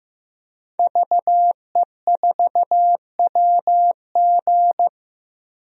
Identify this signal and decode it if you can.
Morse code, heard as a keyed tone, about 15 wpm, 700 Hz. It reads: VE4WG
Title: DXCC-100 Call Signs - 5 Characters - 15wpm